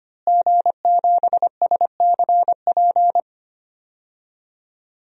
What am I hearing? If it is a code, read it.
Morse code, heard as a keyed tone, about 25 wpm, 700 Hz. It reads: G7SCP
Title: DXCC-100 Call Signs - 5 Characters - Repeated 2x - 25wpm